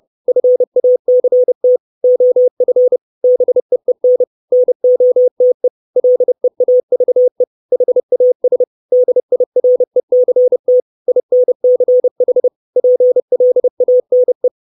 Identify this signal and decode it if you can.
Morse code, heard as a keyed tone, about 30 wpm, 500 Hz. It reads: FACT OF BEEN NOTE LEAVE HAS DIRECT INCH PLANE
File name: 632 Words 30WPM